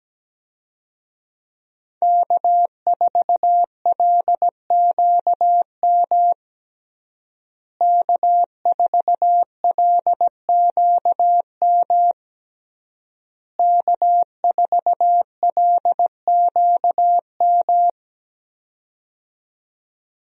Morse code, heard as a keyed tone, about 17 wpm, 700 Hz. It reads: K4LQM K4LQM K4LQM
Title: USA Call Signs - 5 Characters - Repeated 3x - 17wpm